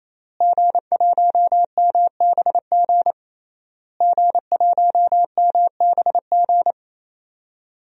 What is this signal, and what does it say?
Morse code, heard as a keyed tone, about 28 wpm, 700 Hz. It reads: G1MBG G1MBG
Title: DXCC-100 Call Signs - 5 Characters - Repeated 3x - 28wpm